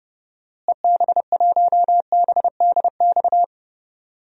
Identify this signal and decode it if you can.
Morse code, heard as a keyed tone, about 30 wpm, 700 Hz. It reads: EB1BDX
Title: DXCC-100 Call Signs - 6 Characters - Repeated 2x - 30wpm